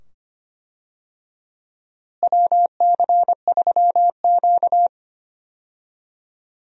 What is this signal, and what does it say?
Morse code, heard as a keyed tone, about 25 wpm, 700 Hz. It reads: WC3Q
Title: USA Call Signs - 4 Characters - Repeated 2x - 25wpm